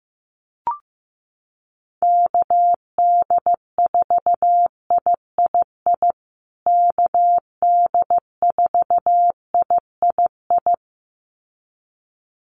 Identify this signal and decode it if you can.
Morse code, heard as a keyed tone, about 15 wpm, 700 Hz. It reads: KD4III KD4III
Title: USA Call Signs Repeated 2x 15wpm